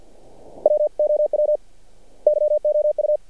Morse code, heard as a keyed tone, about 50 wpm, 600 Hz. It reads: WOW YOU
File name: archived / CRSnet